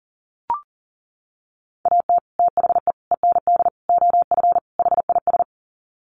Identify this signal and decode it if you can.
Morse code, heard as a keyed tone, about 40 wpm, 700 Hz. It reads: AT THE END OF HIS
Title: Sets of 5 Words Encore 40wpm